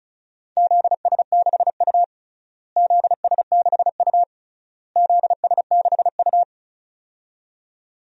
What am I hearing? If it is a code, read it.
Morse code, heard as a keyed tone, about 34 wpm, 700 Hz. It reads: ZS6U ZS6U ZS6U
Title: DXCC-100 Call Signs - 4 Characters - Repeated 3x - 35wpm